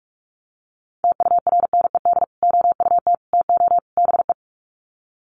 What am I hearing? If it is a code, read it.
Morse code, heard as a keyed tone, about 44 wpm, 700 Hz. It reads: TURNED OUT TO BE
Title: Sets of 4 Words 45wpm